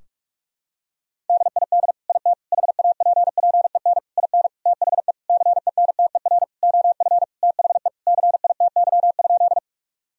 Morse code, heard as a keyed tone, about 41 wpm, 700 Hz. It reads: DID IT HAPPEN IN THE CENTER OF THE CITY?
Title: Sentences from Top 1000 Words - Review Part 2 - 45wpm